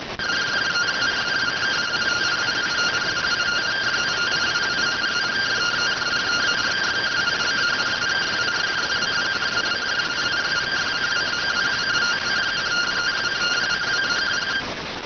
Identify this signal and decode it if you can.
Two-tone teleprinter signal, about 170 Hz shift.